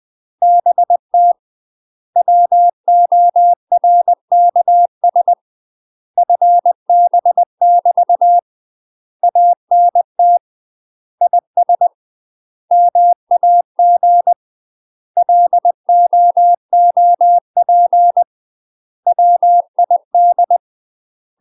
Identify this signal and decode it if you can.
Morse code, heard as a keyed tone, about 20 wpm, 700 Hz. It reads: BT WORKS FB <BT> ANT IS MAG LOOP WID
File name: UK MASTER QSOs TRAINING / 20WPM / UK MASTER QSOs 0022